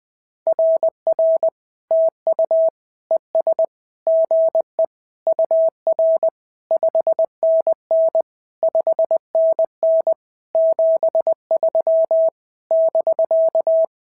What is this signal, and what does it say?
Morse code, heard as a keyed tone, about 19 wpm, 650 Hz. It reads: RR TU ES GE UR 5NN 5NN 73 <BK>
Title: SOTA QSO Practice 20wpm